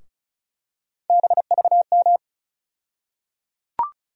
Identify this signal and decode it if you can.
Morse code, heard as a keyed tone, about 36 wpm, 700 Hz. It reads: BVM
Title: USA Call Sign Postfixes - 3 Letters 35wpm